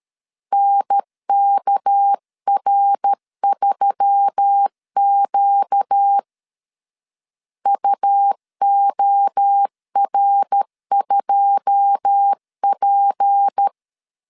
Morse code, heard as a keyed tone, about 13 wpm, 800 Hz. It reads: NKR3Q UOR2P
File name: CW / DET / 1 9 1500C S A 12wpm 3800